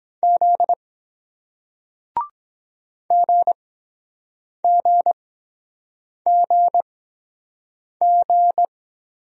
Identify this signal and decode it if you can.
Morse code, heard as a keyed tone, about 23 wpm, spaced as for 7 wpm, 700 Hz. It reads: Z GGGG